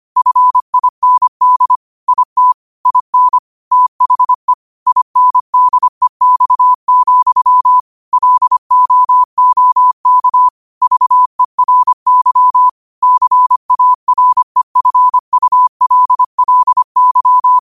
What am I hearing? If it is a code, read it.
Morse code, heard as a keyed tone, about 25 wpm, 1000 Hz. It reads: FIND IT IN THE INDEX, LOOK VERY CAREFULLY
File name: qotdfiles / 25wpm